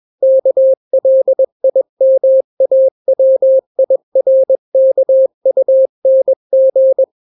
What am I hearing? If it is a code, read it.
Morse code, heard as a keyed tone, about 21 wpm, 530 Hz. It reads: KLIMAWIRKUNG
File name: daily / 20wpm